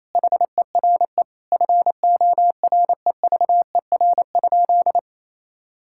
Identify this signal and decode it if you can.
Morse code, heard as a keyed tone, about 28 wpm, 700 Hz. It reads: HERE FOREVER?
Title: Sentences from Top 1600 Words 28wpm